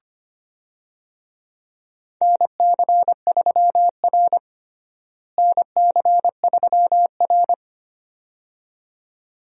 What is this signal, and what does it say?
Morse code, heard as a keyed tone, about 25 wpm, 700 Hz. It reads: NC3R NC3R
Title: USA Call Signs - 4 Characters - Repeated 2x - 25wpm